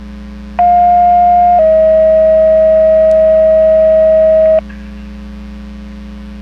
Two-tone page, heard as a one-second tone, then a three-second tone, about 710 then 640 Hz. It is heard from 0.6 to 4.6 s.
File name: Portage 154.385 Paging